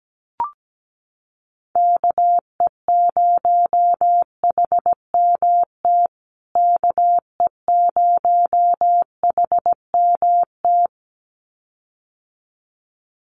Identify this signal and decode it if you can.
Morse code, heard as a keyed tone, about 17 wpm, 700 Hz. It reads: KE0HMT KE0HMT
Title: USA Call Signs Repeated 2x 17wpm